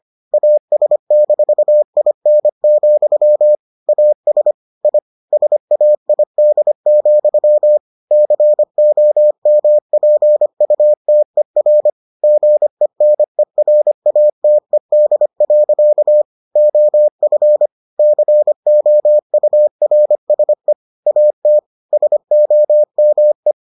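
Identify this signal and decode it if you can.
Morse code, heard as a keyed tone, about 25 wpm, 600 Hz. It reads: AS-IN, AS I SAID, COMPUTER GENERATED. OF COURSE AT SOME
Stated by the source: Morse edition of the Foundations of Amateur Radio episodes.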